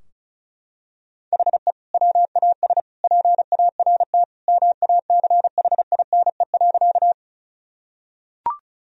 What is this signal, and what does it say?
Morse code, heard as a keyed tone, about 34 wpm, 700 Hz. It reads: HE WAS PART MACHINE.
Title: Sentences from Top 1000 Words - Review Part 2 - 35wpm